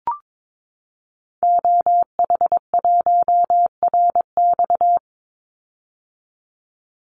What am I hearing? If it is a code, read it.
Morse code, heard as a keyed tone, about 22 wpm, 700 Hz. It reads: OH1RX
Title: Call Signs - 22wpm